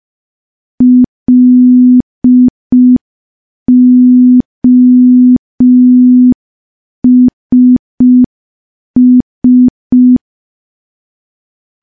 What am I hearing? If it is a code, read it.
Morse code, heard as a keyed tone, about 5 wpm, 250 Hz. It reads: LOSS
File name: morse / morse-training-05WPM